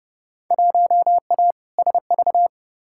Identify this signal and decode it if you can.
Morse code, heard as a keyed tone, about 30 wpm, 700 Hz. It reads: 1A SV
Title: Field Day Exchanges - 30wpm